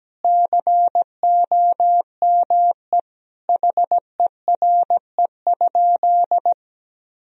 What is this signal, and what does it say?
Morse code, heard as a keyed tone, about 17 wpm, 700 Hz. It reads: COME HERE?
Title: Sentences from Top 1000 Words - Review Part 4 - 17wpm